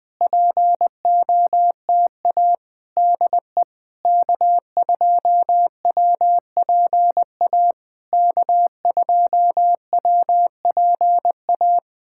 Morse code, heard as a keyed tone, about 20 wpm, 700 Hz. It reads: POTA DE K2WPA K2WPA
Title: POTA QSO Practice - 20wpm